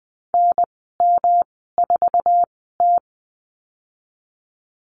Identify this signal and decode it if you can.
Morse code, heard as a keyed tone, about 20 wpm, spaced as for 14 wpm, 700 Hz. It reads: NM4T
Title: 147 Lesson 24g - Letter y - Call Signs 20wpm